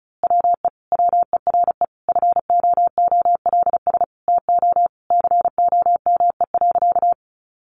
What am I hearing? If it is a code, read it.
Morse code, heard as a keyed tone, about 34 wpm, 700 Hz. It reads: WE WERE FOOLS TO COME.
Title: A Sound Of Thunder 35wpm